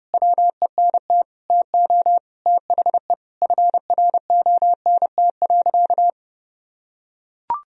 Morse code, heard as a keyed tone, about 30 wpm, 700 Hz. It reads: WENT TO THE FRONT.
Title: Sentences from Top 1000 Words - Review Part 2 - 30wpm